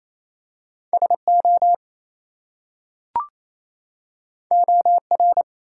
Morse code, heard as a keyed tone, about 28 wpm, 700 Hz. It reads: SO OR
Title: Two Letter Words - 28wpm